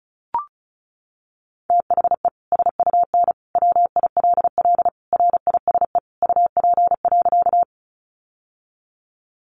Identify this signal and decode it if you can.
Morse code, heard as a keyed tone, about 34 wpm, 700 Hz. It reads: THE SUN WILL RISE UP.
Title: Sentences from Top 900 Words 35wpm